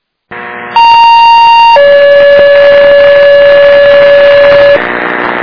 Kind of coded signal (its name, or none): two-tone page